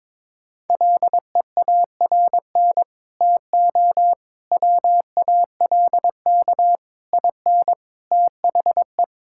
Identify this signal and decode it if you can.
Morse code, heard as a keyed tone, about 22 wpm, 700 Hz. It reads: LEARN TO WALK IN THE
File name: Sets of 9 Words - 22wpm